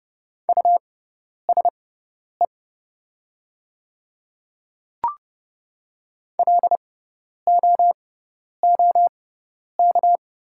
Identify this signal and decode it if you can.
Morse code, heard as a keyed tone, about 30 wpm, spaced as for 10 wpm, 700 Hz. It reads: USE LOOK
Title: 6x Character Spacing - Top 100 Words - 30wpm